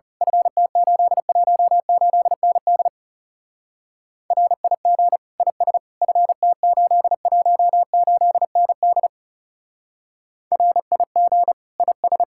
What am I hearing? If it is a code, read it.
Morse code, heard as a keyed tone, about 34 wpm, 700 Hz. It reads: FT818ND RIG IS FT818ND RIG IS
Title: Speed-Racing Common QSO Phrases - 30wpm